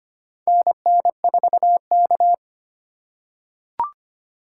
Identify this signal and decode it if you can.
Morse code, heard as a keyed tone, about 25 wpm, 700 Hz. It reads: NN4K
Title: USA Call Signs - 4 Characters - 4x Character Spacing - 25wpm